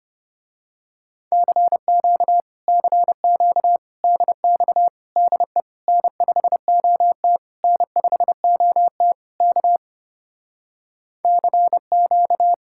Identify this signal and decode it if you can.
Morse code, heard as a keyed tone, about 28 wpm, 700 Hz. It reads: CQ CQ DX DE N5OT N5OT K CQ
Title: Speed-Racing Common QSO Phrases - 25wpm